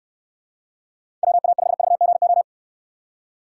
Morse code, heard as a keyed tone, about 69 wpm, 700 Hz. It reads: WA5VGO